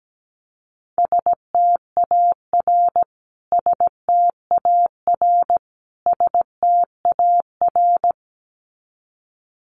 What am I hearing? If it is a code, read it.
Morse code, heard as a keyed tone, about 17 wpm, 700 Hz. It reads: STAR STAR STAR
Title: Top 500 Words Repeated 17wpm